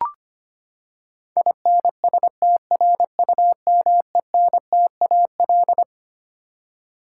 Morse code, heard as a keyed tone, about 25 wpm, 700 Hz. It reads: INSTRUMENTAL
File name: Twelve Letter Words - 25wpm